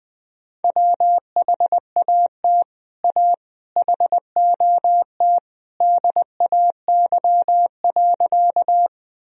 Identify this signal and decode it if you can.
Morse code, heard as a keyed tone, about 21 wpm, 700 Hz. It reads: WHAT A HOT DAY.
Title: Sentences from Top 1000 Words - Review Part 3 - 20wpm